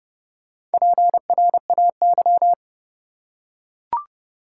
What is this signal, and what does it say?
Morse code, heard as a keyed tone, about 30 wpm, 700 Hz. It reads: PRAY
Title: Four Letter Words - 30wpm